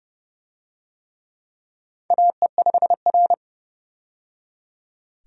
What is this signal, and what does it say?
Morse code, heard as a keyed tone, about 30 wpm, 700 Hz. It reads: AE5R
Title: USA Call Signs - 4 Characters - 30wpm